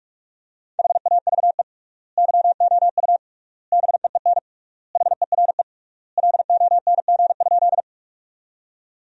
Morse code, heard as a keyed tone, about 46 wpm, 700 Hz. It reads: HAVE YOU BEEN HERE LONG?
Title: 3x Word-Spacing - Sentences from Top 200 Words - 45wpm